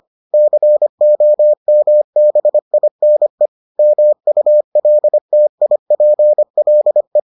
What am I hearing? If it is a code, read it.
Morse code, heard as a keyed tone, about 25 wpm, 600 Hz. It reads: COMBINE MULTIPLE